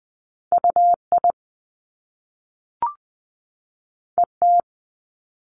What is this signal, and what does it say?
Morse code, heard as a keyed tone, about 20 wpm, 700 Hz. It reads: UI ET
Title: Top Two Letter Combinations in Words 20wpm